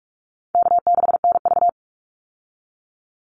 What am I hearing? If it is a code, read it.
Morse code, heard as a keyed tone, about 44 wpm, 700 Hz. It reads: K6NV
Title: Easy Call Signs Repeated 3x - 45wpm